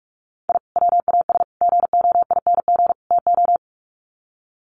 Morse code, heard as a keyed tone, about 44 wpm, 700 Hz. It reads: I WAS GOING TO
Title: Sets of 4 Words 45wpm